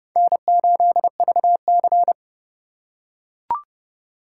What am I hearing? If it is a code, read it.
Morse code, heard as a keyed tone, about 29 wpm, 700 Hz. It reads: N8VC